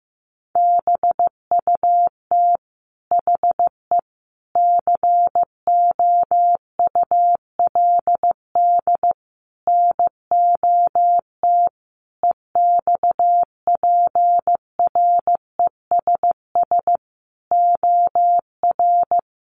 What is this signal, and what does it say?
Morse code, heard as a keyed tone, about 15 wpm, 700 Hz. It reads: BUT HE COULD NOT EXPRESS OR
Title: Sentences from Top 1100 Words 15wpm